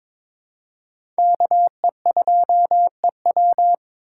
Morse code, heard as a keyed tone, about 22 wpm, 700 Hz. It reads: KE2EW
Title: USA Call Signs - 5 Characters - 8x Character Spacing - 22wpm